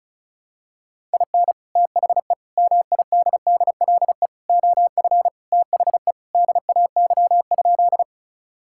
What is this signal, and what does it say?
Morse code, heard as a keyed tone, about 34 wpm, 700 Hz. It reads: IN THE MIDDLE OF THE DAY?
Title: Sentences from Top 1000 Words - Review Part 6 - 35wpm